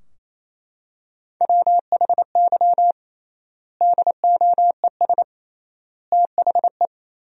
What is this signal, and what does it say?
Morse code, heard as a keyed tone, about 28 wpm, 700 Hz. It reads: WHY DOES THE
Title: 3x Word-Spacing - Sentences from Top 500 Words - 28wpm